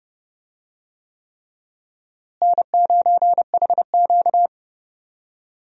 Morse code, heard as a keyed tone, about 30 wpm, 700 Hz. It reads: N9HQ
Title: USA Call Signs - 4 Characters - Repeated 3x - 30wpm